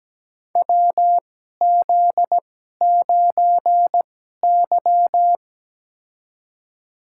Morse code, heard as a keyed tone, about 17 wpm, spaced as for 12 wpm, 700 Hz. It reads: WZ9Y
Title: USA Call Signs - 4 Characters - 2x Character Spacing - 17wpm